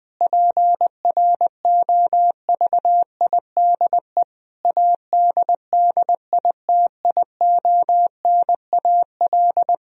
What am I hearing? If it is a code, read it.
Morse code, heard as a keyed tone, about 19 wpm, 700 Hz. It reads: PROVIDE ADDITIONAL
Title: Sentences from Top 1000 Words - Review Part 4 - 20wpm